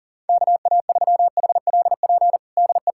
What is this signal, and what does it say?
Morse code, heard as a keyed tone, about 37 wpm, 700 Hz. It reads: KA3HLP DE
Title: USA Callsign DE Callsign 40wpm